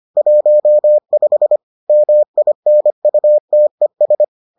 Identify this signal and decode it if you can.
Morse code, heard as a keyed tone, about 25 wpm, 600 Hz. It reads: 15 MINUTES